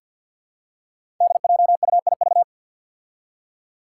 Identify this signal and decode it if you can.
Morse code, heard as a keyed tone, about 53 wpm, 700 Hz. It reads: DJUIV
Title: Five Challenging Letter-Number Combinations 50wpm